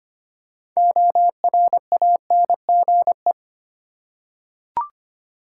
Morse code, heard as a keyed tone, about 25 wpm, 700 Hz. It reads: ORANGE
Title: Six Letter Words - 25wpm